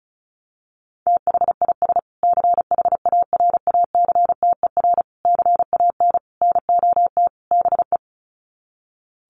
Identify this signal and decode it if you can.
Morse code, heard as a keyed tone, about 34 wpm, 700 Hz. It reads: THIS CHARACTER CAN NOT BE
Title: Sets of 5 Words Encore2 35wpm